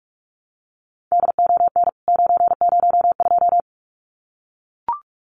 Morse code, heard as a keyed tone, about 44 wpm, 700 Hz. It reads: DON 902
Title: CWT Exchange 45wpm